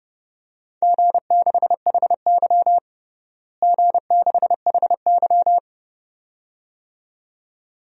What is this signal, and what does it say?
Morse code, heard as a keyed tone, about 30 wpm, 700 Hz. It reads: G6HY G6HY
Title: DXCC-100 Call Signs - 4 Characters - Repeated 2x - 30wpm